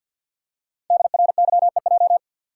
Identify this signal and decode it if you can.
Morse code, heard as a keyed tone, about 50 wpm, 690 Hz. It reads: DRYEJ
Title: Code Groups - 5 Letters - 50wpm